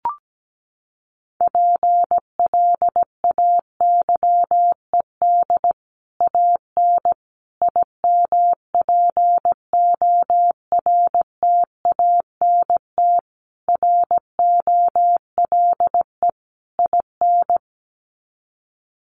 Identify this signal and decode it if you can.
Morse code, heard as a keyed tone, about 17 wpm, 700 Hz. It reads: PLAYED AN IMPORTANT ROLE IN
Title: Sets of 5 Words Encore2 17wpm